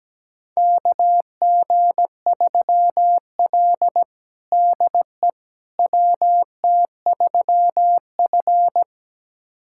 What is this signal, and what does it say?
Morse code, heard as a keyed tone, about 17 wpm, 700 Hz. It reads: KG3L DE WT3F
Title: USA Callsign DE Callsign - 4 Characters - 17wpm